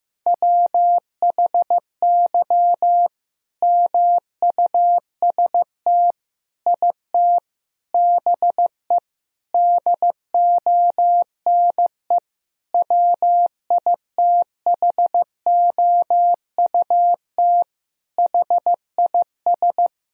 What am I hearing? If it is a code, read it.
Morse code, heard as a keyed tone, about 15 wpm, 700 Hz. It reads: WHY MUST IT BE DONE WITHOUT HIS
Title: Sentences from Top 1000 Words - Review Part 3 - 15wpm